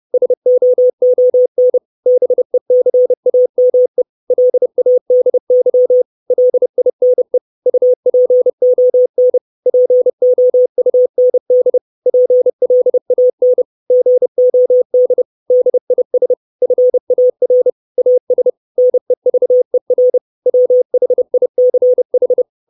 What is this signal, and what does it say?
Morse code, heard as a keyed tone, about 30 wpm, 500 Hz. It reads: SOON BECAME LADY LINE UPON POUND PLAN GOD DIS FAR AS NEVER WHICH
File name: 632 Words 30WPM